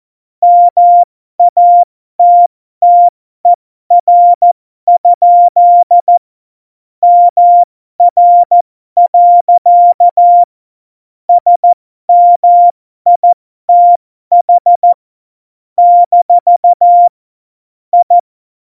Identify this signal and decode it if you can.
Morse code, heard as a keyed tone, about 14 wpm, spaced as for 12 wpm, 700 Hz. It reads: MATTER? MR. SMITH - I
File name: Morsekurs / Witze